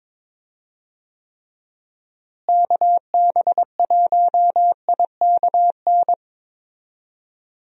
Morse code, heard as a keyed tone, about 22 wpm, 700 Hz. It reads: KB1IKN
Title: USA Call Signs - 6 Characters - Repeated 2x - 22wpm